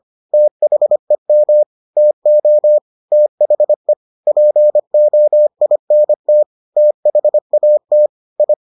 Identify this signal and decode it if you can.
Morse code, heard as a keyed tone, about 25 wpm, 600 Hz. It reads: THEM TO THE POINT THAT I